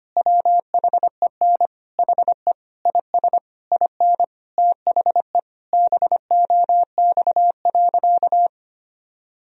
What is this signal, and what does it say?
Morse code, heard as a keyed tone, about 25 wpm, 700 Hz. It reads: WHEN HE IS IN THE BOX.
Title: Sentences from Top 1000 Words - Review Part 2 - 25wpm